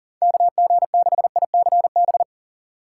Morse code, heard as a keyed tone, about 40 wpm, 700 Hz. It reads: KG6ICB
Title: USA Call Signs - 2x Character Spacing - 40wpm